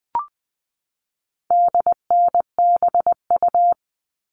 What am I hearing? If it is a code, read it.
Morse code, heard as a keyed tone, about 20 wpm, 700 Hz. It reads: DNBU
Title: All Cut Numbers - 4-Digit 20wpm